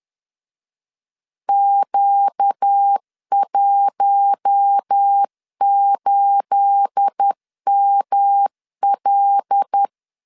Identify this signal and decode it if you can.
Morse code, heard as a keyed tone, about 10 wpm, 800 Hz. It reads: Q18ML
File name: LEZIONI / 9 28 10wpm 2842